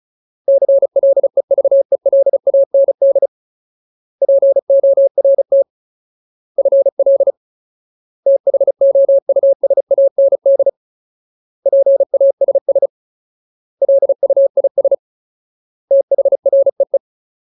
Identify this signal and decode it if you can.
Morse code, heard as a keyed tone, about 34 wpm, 550 Hz. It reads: CLEVELAND PORT FL THOUSAND PASS LUIS THREE
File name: Ultimate Word List / 35WPM Ultimate